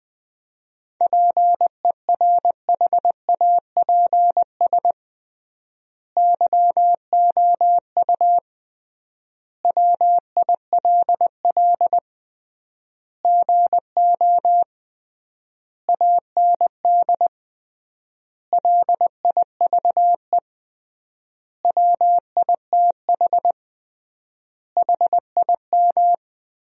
Morse code, heard as a keyed tone, about 20 wpm, 700 Hz. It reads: PERHAPS YOU WILL GO AND LIVE WITH HIM
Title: 3x Word-Spacing - Sentences from Top 500 Words - 20wpm